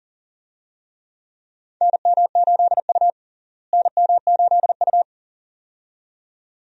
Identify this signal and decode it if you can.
Morse code, heard as a keyed tone, about 40 wpm, 700 Hz. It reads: NM8U NM8U
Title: USA Call Signs - 4 Characters - Repeated 2x - 40wpm